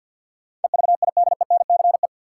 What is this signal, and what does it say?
Morse code, heard as a keyed tone, about 53 wpm, 700 Hz. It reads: EVIDENCE
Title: Sets of 5 Words Encore - New Word Drill - 50wpm